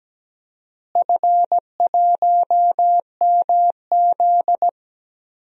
Morse code, heard as a keyed tone, about 17 wpm, 700 Hz. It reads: F1MZ
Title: DXCC-100 Call Signs - 4 Characters - Repeated 3x - 17wpm